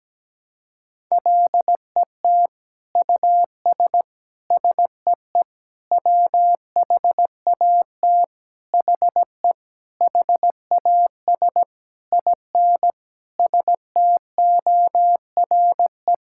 Morse code, heard as a keyed tone, about 17 wpm, 700 Hz. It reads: LET US SEE WHAT HE HAS IN STORE
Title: Sentences from Top 1000 Words - Review Part 5 - 17wpm